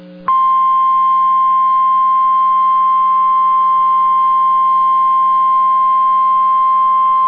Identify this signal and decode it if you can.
Emergency Alert System (SAME) tones (the 1050 Hz attention tone). It is heard from 1.1 to 6.1 s.